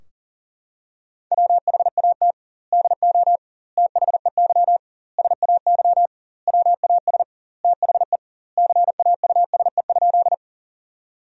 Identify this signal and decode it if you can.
Morse code, heard as a keyed tone, about 40 wpm, 700 Hz. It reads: WHAT DO THEY SAY WAS THE CAUSE?
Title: 2x Word-Spacing - Sentences from Top 200 Words - 40wpm